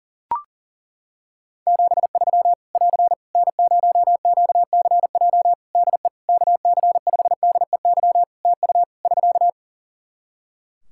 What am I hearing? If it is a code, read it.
Morse code, heard as a keyed tone, about 40 wpm, 700 Hz. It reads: 73 <AR> N0QCJ DE KC5DEY TU <SK>
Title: Practice QSO - Exchange 4 - Ending - 40wpm